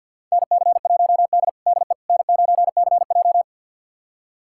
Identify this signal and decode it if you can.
Morse code, heard as a keyed tone, about 49 wpm, 700 Hz. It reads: NC1D DE N9CJ
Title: USA Callsign DE Callsign - 4 Characters - 50wpm